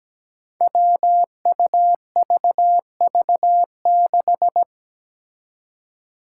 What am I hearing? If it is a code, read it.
Morse code, heard as a keyed tone, about 17 wpm, 700 Hz. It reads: WUVV6
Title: Five Challenging Letter-Number Combinations 17wpm